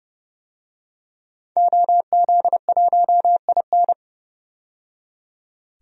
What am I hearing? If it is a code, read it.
Morse code, heard as a keyed tone, about 32 wpm, 700 Hz. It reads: OZ1IN
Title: DXCC-100 Call Signs - 5 Characters - 30wpm